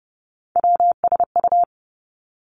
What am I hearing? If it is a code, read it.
Morse code, heard as a keyed tone, about 32 wpm, 700 Hz. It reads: WSU
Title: Three Challenging Letter-Number Combinations 30wpm